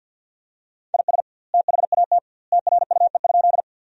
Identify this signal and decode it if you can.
Morse code, heard as a keyed tone, about 50 wpm, 700 Hz. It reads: IS THAT TRUE?
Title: 2x Word-Spacing - Sentences from Top 400 Words - 50wpm